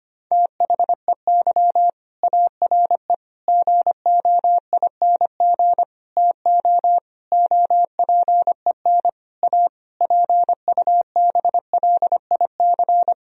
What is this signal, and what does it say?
Morse code, heard as a keyed tone, about 25 wpm, 700 Hz. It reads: THEY ARE GOING TO OPEN A PUBLIC
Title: Sentences from Top 1000 Words - Review Part 3 - 25wpm